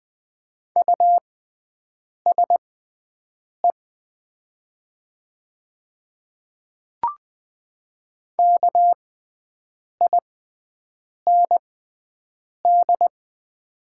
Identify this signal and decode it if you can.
Morse code, heard as a keyed tone, about 20 wpm, spaced as for 7 wpm, 700 Hz. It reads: USE KIND
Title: 6x Character Spacing - Top 500 Words - 20wpm